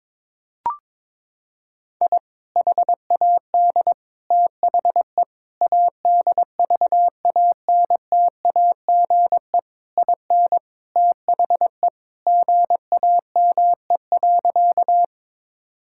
Morse code, heard as a keyed tone, about 22 wpm, 700 Hz. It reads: I HAD THE ADVANTAGE IN THE GAME.
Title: Sentences from Top 1000 Words - Review Part 6 - 22wpm